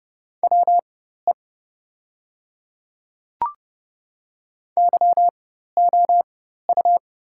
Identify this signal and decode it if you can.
Morse code, heard as a keyed tone, about 30 wpm, spaced as for 14 wpm, 700 Hz. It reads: WE YOU
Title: 4x Character Spacing - Top 100 Words - 30wpm